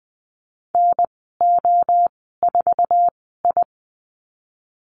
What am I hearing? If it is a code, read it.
Morse code, heard as a keyed tone, about 20 wpm, spaced as for 15 wpm, 700 Hz. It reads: NO4I
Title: 128 Lesson 21g - Number 6 - Call Signs 20wpm